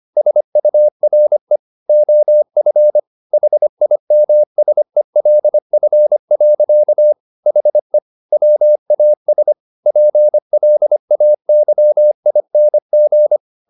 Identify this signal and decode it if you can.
Morse code, heard as a keyed tone, about 25 wpm, 600 Hz. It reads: SURE OF HIMSELF. HE WAS PLAYING